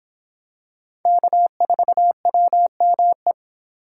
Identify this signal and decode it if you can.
Morse code, heard as a keyed tone, about 26 wpm, 700 Hz. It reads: K4WME